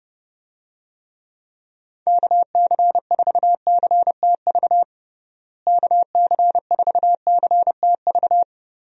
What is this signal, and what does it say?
Morse code, heard as a keyed tone, about 30 wpm, 700 Hz. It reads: KC4CTV KC4CTV
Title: USA Call Signs - 6 Characters - Repeated 2x - 30wpm